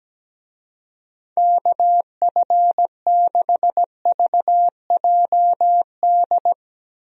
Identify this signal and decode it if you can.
Morse code, heard as a keyed tone, about 17 wpm, 700 Hz. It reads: KF6VJD
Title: USA Call Signs - 6 Characters - 17wpm